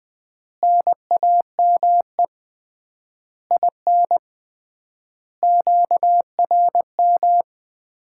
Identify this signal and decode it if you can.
Morse code, heard as a keyed tone, about 20 wpm, 700 Hz. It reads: NAME IN QRM
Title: Common QSO Phrases - 3x Word-Spacing - 20wpm